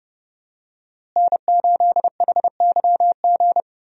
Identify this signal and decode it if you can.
Morse code, heard as a keyed tone, about 30 wpm, 700 Hz. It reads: N8HYG
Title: USA Call Signs - 5 Characters - Repeated 2x - 30wpm